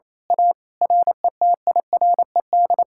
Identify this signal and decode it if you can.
Morse code, heard as a keyed tone, about 27 wpm, 700 Hz. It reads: A RETIRED
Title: Speed-Racing Common QSO Phrases Encore - 28wpm